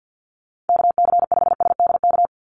Morse code, heard as a keyed tone, about 48 wpm, 700 Hz. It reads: KC5SDK
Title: USA Call Signs 50wpm